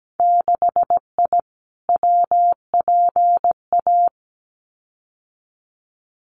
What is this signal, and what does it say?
Morse code, heard as a keyed tone, about 17 wpm, 700 Hz. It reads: 6I WPA
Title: Winter Field Day Exchanges Encore 17wpm